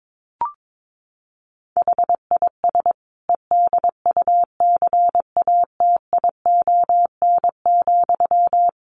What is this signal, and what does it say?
Morse code, heard as a keyed tone, about 22 wpm, 700 Hz. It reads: HIS EDUCATION,
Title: Sentences from Top 1400 Words 22wpm